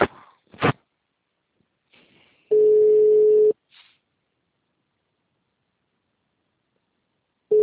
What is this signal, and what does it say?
Ringback tone (Europe, 425 Hz).